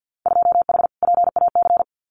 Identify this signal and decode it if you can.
Morse code, heard as a keyed tone, about 50 wpm, 700 Hz. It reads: 2H PAC
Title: Winter Field Day Exchanges Encore 50wpm